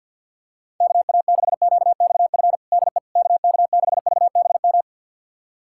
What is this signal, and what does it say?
Morse code, heard as a keyed tone, about 51 wpm, 700 Hz. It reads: KA6QXF DE KK6VBM
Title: USA Callsign DE Callsign - 6 Characters - 50wpm